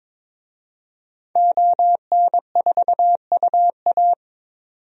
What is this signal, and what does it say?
Morse code, heard as a keyed tone, about 22 wpm, 700 Hz. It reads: ON4UA
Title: DXCC-100 Call Signs - 5 Characters - 22wpm